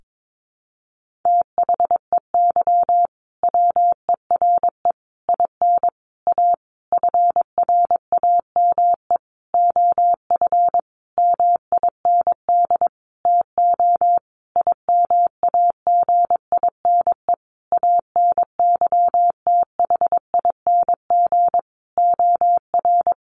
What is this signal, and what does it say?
Morse code, heard as a keyed tone, about 22 wpm, 700 Hz. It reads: THEY WERE IN A FRAME OF MIND TO IMAGINE ANYTHING OR
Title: Sentences from Top 1400 Words 22wpm